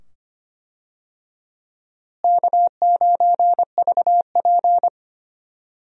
Morse code, heard as a keyed tone, about 25 wpm, 700 Hz. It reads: K9VP
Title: USA Call Signs - 4 Characters - Repeated 2x - 25wpm